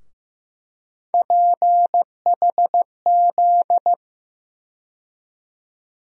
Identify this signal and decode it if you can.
Morse code, heard as a keyed tone, about 15 wpm, 700 Hz. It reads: PHZ
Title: USA Call Sign Postfixes - 3 Letters 15wpm